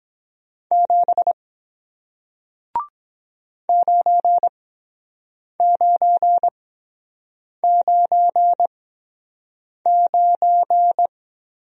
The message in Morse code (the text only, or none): 7 9999